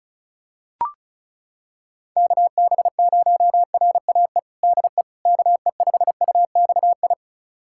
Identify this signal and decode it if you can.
Morse code, heard as a keyed tone, about 36 wpm, 700 Hz. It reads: KB0RAE DE KE5UXI
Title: USA Callsign DE Callsign 35wpm